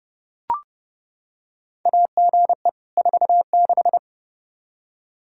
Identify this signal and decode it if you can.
Morse code, heard as a keyed tone, about 30 wpm, 700 Hz. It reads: AGE 46
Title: Practice QSO - Exchange 3 - More Info - 30wpm